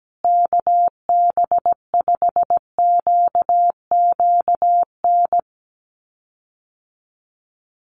Morse code, heard as a keyed tone, about 17 wpm, 700 Hz. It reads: KB5QQN
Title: USA Call Signs Repeated 3x 17wpm